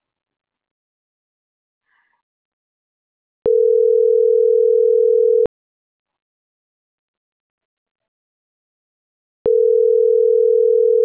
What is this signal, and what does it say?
Ringback tone (North America), 440 + 480 Hz.